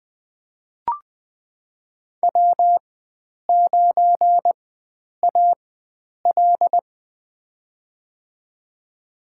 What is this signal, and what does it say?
Morse code, heard as a keyed tone, about 21 wpm, spaced as for 9 wpm, 700 Hz. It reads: W9AL
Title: USA Call Signs - 4 Characters - 4x Character Spacing - 20wpm